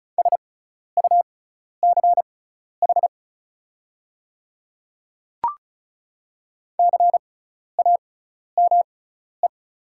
Morse code, heard as a keyed tone, about 36 wpm, spaced as for 12 wpm, 700 Hz. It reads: SUCH CAME
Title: 6x Character Spacing - Top 500 Words - 35wpm